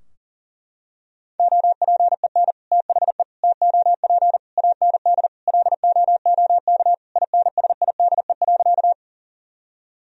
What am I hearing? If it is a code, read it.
Morse code, heard as a keyed tone, about 40 wpm, 700 Hz. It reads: OPEN THE TOP AND LOOK INSIDE.
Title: Sentences from Top 1000 Words - Review Part 2 - 40wpm